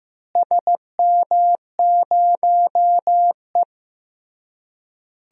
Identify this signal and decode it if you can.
Morse code, heard as a keyed tone, about 16 wpm, 700 Hz. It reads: SM0E
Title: DXCC-100 Call Signs - 4 Characters - Repeated 3x - 15wpm